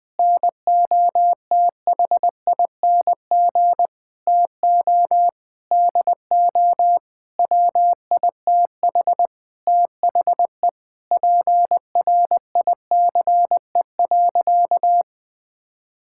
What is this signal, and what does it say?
Morse code, heard as a keyed tone, about 20 wpm, 700 Hz. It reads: NOTHING TO DO WITH THE PRICE.
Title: Sentences with 7-Word Phrases - Spoken Beginning - 20wpm